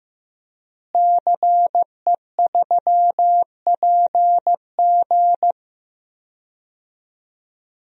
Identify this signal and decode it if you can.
Morse code, heard as a keyed tone, about 15 wpm, 700 Hz. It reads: CE3PG
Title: DXCC-100 Call Signs - 5 Characters - Repeated 2x - 15wpm